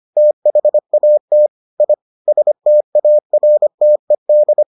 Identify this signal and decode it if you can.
Morse code, heard as a keyed tone, about 25 wpm, 600 Hz. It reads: THAT I STARTED